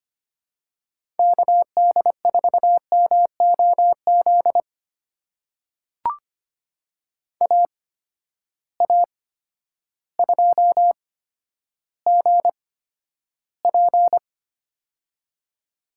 Morse code, heard as a keyed tone, about 25 wpm, spaced as for 11 wpm, 700 Hz. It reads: KD4MOZ AA2GP
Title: USA Call Signs - 8x Character Spacing - 25wpm